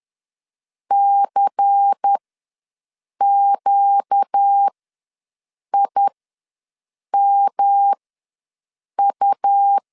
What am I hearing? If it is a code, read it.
Morse code, heard as a keyed tone, about 10 wpm, spaced as for 6 wpm, 800 Hz. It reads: CQIMU
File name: LEZIONI / 5 LBVC / 5 LBVC ws 10wpm 3733